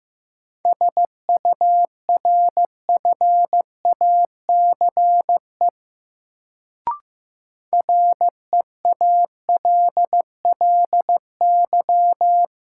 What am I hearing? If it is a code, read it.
Morse code, heard as a keyed tone, about 15 wpm, 700 Hz. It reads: SURFACE REALLY
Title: Sets of 4 Words - New Word Drill - 15wpm